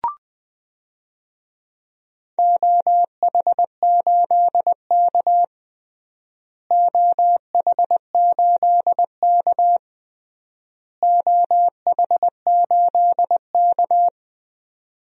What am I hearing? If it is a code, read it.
Morse code, heard as a keyed tone, about 20 wpm, 700 Hz. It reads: OH8K OH8K OH8K
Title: DXCC-100 Call Signs - 4 Characters - Repeated 3x - 20wpm